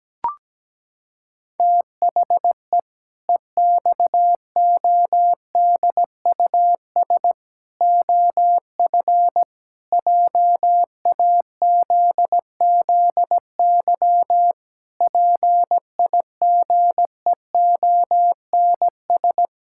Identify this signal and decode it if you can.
Morse code, heard as a keyed tone, about 17 wpm, 700 Hz. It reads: THE EXODUS OF JAZZY PIGEONS
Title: Pangram Challenge v2 - 17wpm